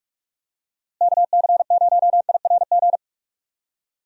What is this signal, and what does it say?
Morse code, heard as a keyed tone, about 46 wpm, 700 Hz. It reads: KC0IRG
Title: USA Call Signs - 6 Characters - Repeated 2x - 45wpm